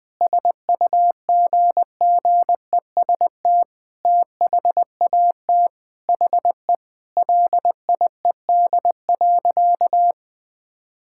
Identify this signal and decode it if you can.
Morse code, heard as a keyed tone, about 20 wpm, 700 Hz. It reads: SUGGEST THAT HE LIED.
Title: Sentences with 7-Word Phrases - Spoken Beginning - 20wpm